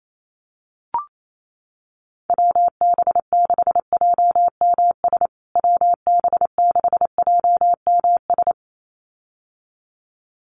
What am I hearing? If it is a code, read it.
Morse code, heard as a keyed tone, about 28 wpm, 700 Hz. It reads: WB6JMS WB6JMS
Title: USA Call Signs Repeated 2x 28wpm